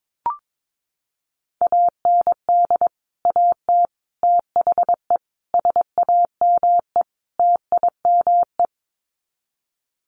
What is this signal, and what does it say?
Morse code, heard as a keyed tone, about 22 wpm, 700 Hz. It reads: AND AT THE SAME TIME
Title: Sets of 5 Words 22wpm